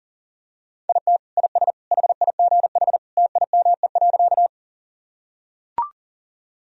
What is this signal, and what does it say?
Morse code, heard as a keyed tone, about 37 wpm, 700 Hz. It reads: IT IS HIGH TIME.
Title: Sentences from Top 1000 Words - Review Part 1 - 40wpm